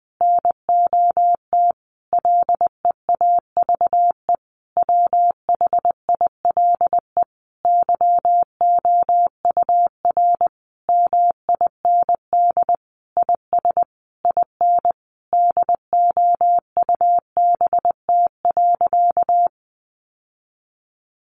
Morse code, heard as a keyed tone, about 20 wpm, 700 Hz. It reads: NOT LEAVE WHILE YOUR MIND IS IN DOUBT.
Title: Sentences from Top 1500 Words 20wpm